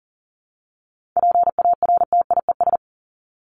Morse code, heard as a keyed tone, about 40 wpm, 700 Hz. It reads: PARTIES
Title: ES:S Suffix - Top 100 Words with Base Words 40wpm